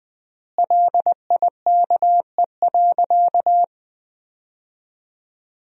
Morse code, heard as a keyed tone, about 20 wpm, 700 Hz. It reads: LIKE.
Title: 4x Word-Spacing - Sentences from Top 500 Words - 20wpm